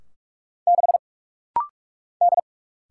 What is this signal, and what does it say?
Morse code, heard as a keyed tone, about 44 wpm, 700 Hz. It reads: 6 D
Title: Single Letter-Number Alt - Rapid-Fire 45wpm